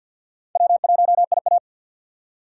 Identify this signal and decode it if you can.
Morse code, heard as a keyed tone, about 49 wpm, 700 Hz. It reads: W1IA
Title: USA Call Signs - 4 Characters - 6x Character Spacing - 50wpm